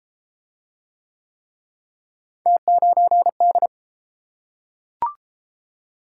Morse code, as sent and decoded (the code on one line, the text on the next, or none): - ----. -..
T9D